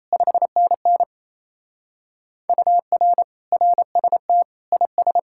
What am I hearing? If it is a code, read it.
Morse code, heard as a keyed tone, about 35 wpm, 700 Hz. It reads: 5NN UR RST IS
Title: Speed-Racing Common QSO Phrases Encore - 28wpm